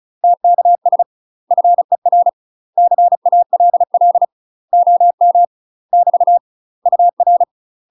Morse code, heard as a keyed tone, about 36 wpm, 700 Hz. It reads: TKS FER CALL OM <BT> UR
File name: UK MASTER QSOs TRAINING / 35WPM / UK MASTER QSOs 0027